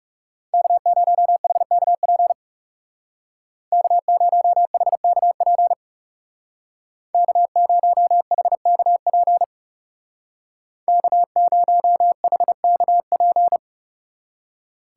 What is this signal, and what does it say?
Morse code, heard as a keyed tone, about 37 wpm, 700 Hz. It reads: K0HKP K0HKP K0HKP K0HKP
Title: Speed Racing - USA Call Signs 30wpm